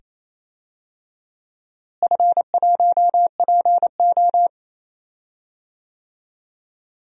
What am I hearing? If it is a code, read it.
Morse code, heard as a keyed tone, about 28 wpm, 700 Hz. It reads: F1PO
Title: DXCC-100 Call Signs - 4 Characters - Repeated 2x - 28wpm